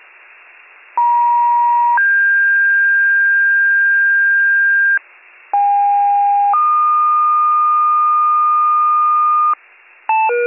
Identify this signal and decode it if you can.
Two-tone page, heard as a one-second tone, then a three-second tone, about 950 then 1600 Hz, about 810 then 1220 Hz, with noise. It is heard from 1.0 to 9.5 s.